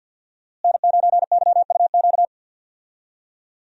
Morse code, heard as a keyed tone, about 52 wpm, 700 Hz. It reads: N9YUQ
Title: USA Call Signs - 5 Characters - Repeated 2x - 50wpm